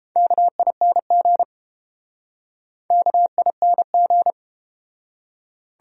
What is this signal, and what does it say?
Morse code, heard as a keyed tone, about 31 wpm, 700 Hz. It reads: KING KING